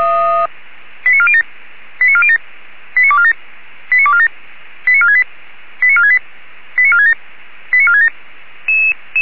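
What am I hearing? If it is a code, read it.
Five-tone selective call (ZVEI), heard as quick runs of five tones, tones of about 70 ms. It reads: 89407 89407 89247 89247 89468 89468 89467 89467
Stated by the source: ZVEI five-tone selective call received by SDR.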